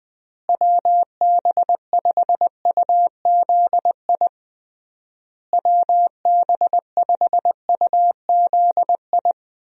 Morse code, heard as a keyed tone, about 19 wpm, 700 Hz. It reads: WB5UZI WB5UZI
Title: USA Call Signs - 6 Characters - Repeated 3x - 20wpm